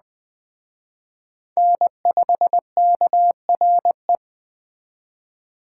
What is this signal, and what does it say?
Morse code, heard as a keyed tone, about 20 wpm, 700 Hz. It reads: N5KRE
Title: Speed Racing - USA 1x3 Call Signs 17wpm